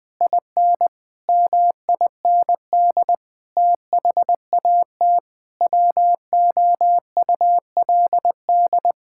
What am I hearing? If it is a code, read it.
Morse code, heard as a keyed tone, about 20 wpm, 700 Hz. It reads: IN MIND THAT WOULD
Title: Sentences with 9-Word Phrases - Spoken Beginning - 20wpm